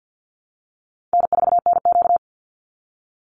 Morse code, heard as a keyed tone, about 50 wpm, 700 Hz. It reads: N4NQ
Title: Call Signs Repeated 3x - 50wpm